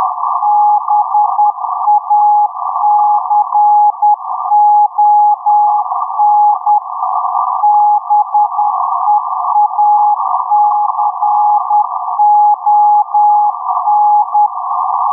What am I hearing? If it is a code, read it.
Morse code, heard as a keyed tone, about 10 wpm, 830 Hz, with noise. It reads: BACON BEACON
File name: blog / mypics / 6953.5-USB-22-Jun-2021-03.10.27